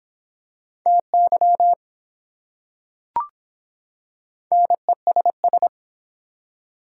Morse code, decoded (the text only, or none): TY NESS